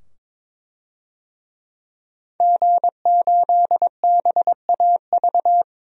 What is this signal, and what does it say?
Morse code, heard as a keyed tone, about 22 wpm, 700 Hz. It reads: G8BAV
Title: DXCC-100 Call Signs - 5 Characters - 22wpm